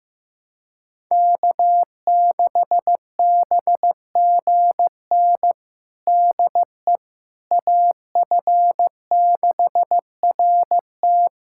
Morse code, heard as a keyed tone, about 15 wpm, 700 Hz. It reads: K6BGN DE AF6RT
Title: USA Callsign DE Callsign - 5 Characters - 15wpm